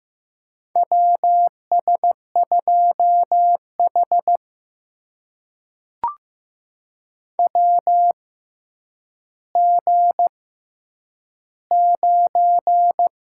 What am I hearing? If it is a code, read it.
Morse code, heard as a keyed tone, about 15 wpm, spaced as for 9 wpm, 700 Hz. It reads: WS2H WG9
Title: USA Call Signs - 4 Characters - 6x Character Spacing - 15wpm